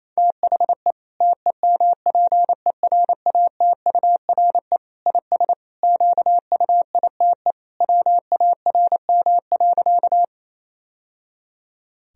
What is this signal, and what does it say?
Morse code, heard as a keyed tone, about 27 wpm, 700 Hz. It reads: THE TEMPERATURE IS QUITE WARM.
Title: Sentences from Top 1000 Words - Review Part 4 - 28wpm